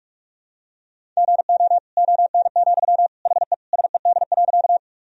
Morse code, heard as a keyed tone, about 47 wpm, 700 Hz. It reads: GO ON, HE SED.
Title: Sentences from Top 100 Words and Abbreviations 45wpm